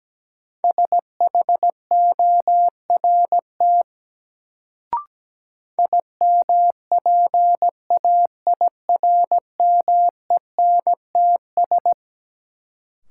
Morse code, heard as a keyed tone, about 17 wpm, 700 Hz. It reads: SHORT IMPAIRMENTS
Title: Sets of 5 Words Encore - New Word Drill - 17wpm